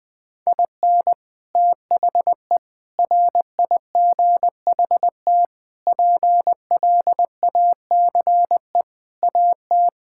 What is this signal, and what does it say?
Morse code, heard as a keyed tone, about 20 wpm, 700 Hz. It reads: IN THE RIGHT PLACE AT
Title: Sets of 8 Words - 20wpm